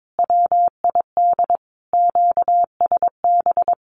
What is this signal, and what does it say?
Morse code, heard as a keyed tone, about 22 wpm, 700 Hz. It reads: WID QSB
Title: Common QSO Phrases 22wpm